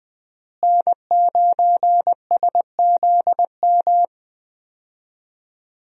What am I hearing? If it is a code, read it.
Morse code, heard as a keyed tone, about 20 wpm, 700 Hz. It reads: N9SZM
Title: USA Call Signs - 5 Characters - Repeated 3x - 20wpm